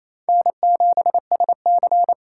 Morse code, heard as a keyed tone, about 36 wpm, 700 Hz. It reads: N7SC